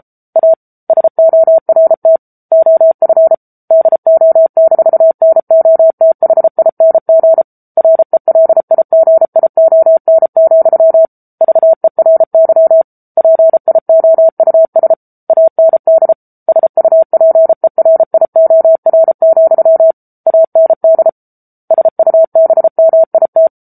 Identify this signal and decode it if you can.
Morse code, heard as a keyed tone, about 34 wpm, 650 Hz. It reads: A SORT OF DO-NOTHING RELIGION, VERY PIOUS AND SUPERIOR, AND SUBMIT
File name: Book Two / 07-Chapter 7 - 33 WPM